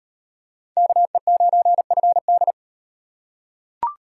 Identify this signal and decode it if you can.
Morse code, heard as a keyed tone, about 37 wpm, 700 Hz. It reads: KE9FD